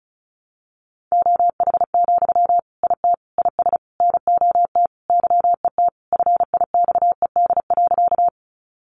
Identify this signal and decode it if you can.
Morse code, heard as a keyed tone, about 34 wpm, 700 Hz. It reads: OH, IT IS NOT YET FIXED.
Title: Sentences from Top 1100 Words 35wpm